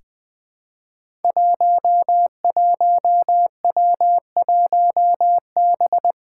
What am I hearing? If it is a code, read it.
Morse code, heard as a keyed tone, about 20 wpm, 700 Hz. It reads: 11W1B
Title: Five Challenging Letter-Number Combinations 20wpm